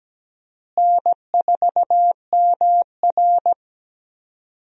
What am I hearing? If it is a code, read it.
Morse code, heard as a keyed tone, about 17 wpm, 700 Hz. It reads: N4MR
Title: USA Call Signs - 4 Characters - Repeated 3x - 17wpm